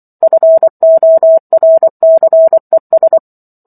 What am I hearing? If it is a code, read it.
Morse code, heard as a keyed tone, about 24 wpm, 650 Hz. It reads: FORCES
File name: Chapter 15 - 24 WPM